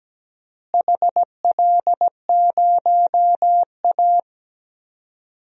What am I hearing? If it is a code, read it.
Morse code, heard as a keyed tone, about 17 wpm, 700 Hz. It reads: HL0A
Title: DXCC-100 Call Signs - 4 Characters - Repeated 2x - 17wpm